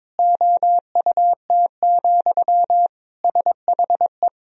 Morse code, heard as a keyed tone, about 22 wpm, 700 Hz. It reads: OUT, SHE
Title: Sentences from Top 1000 Words - Review Part 3 - 22wpm